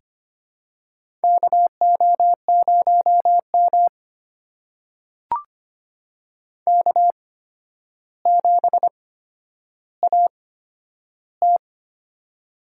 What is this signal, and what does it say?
Morse code, heard as a keyed tone, about 25 wpm, spaced as for 11 wpm, 700 Hz. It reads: KO0M K7AT
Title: USA Call Signs - 4 Characters - 8x Character Spacing - 25wpm